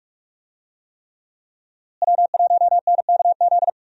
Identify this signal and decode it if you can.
Morse code, heard as a keyed tone, about 44 wpm, 700 Hz. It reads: W1NKZ
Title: USA Call Signs - 5 Characters - Repeated 3x - 45wpm